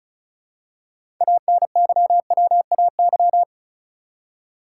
Morse code, heard as a keyed tone, about 36 wpm, 700 Hz. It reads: ANYWAY
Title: Six Letter Words - 35wpm